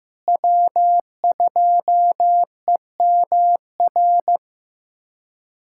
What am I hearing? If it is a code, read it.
Morse code, heard as a keyed tone, about 15 wpm, 700 Hz. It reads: W2EMR
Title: USA Callsign DE Callsign - 5 Characters - 15wpm